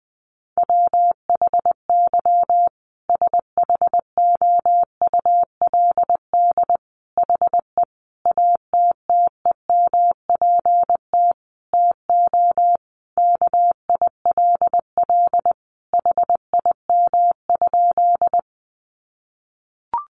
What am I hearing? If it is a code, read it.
Morse code, heard as a keyed tone, about 20 wpm, 700 Hz. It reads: WHY SHOULD HE ATTEMPT TO KILL HIM?
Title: Sentences from Top 1400 Words 20wpm